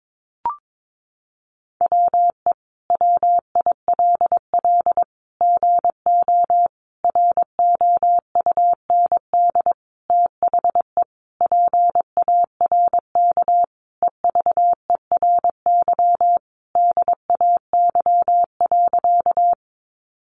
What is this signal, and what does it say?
Morse code, heard as a keyed tone, about 22 wpm, 700 Hz. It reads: WE WILL GO ROUND THE PARK EVERY DAY.
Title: Sentences from Top 1300 Words 22wpm